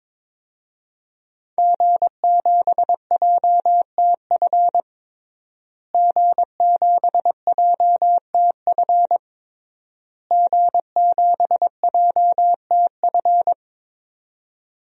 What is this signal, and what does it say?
Morse code, heard as a keyed tone, about 22 wpm, 700 Hz. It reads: G7JTF G7JTF G7JTF
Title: DXCC-100 Call Signs - 5 Characters - Repeated 3x - 22wpm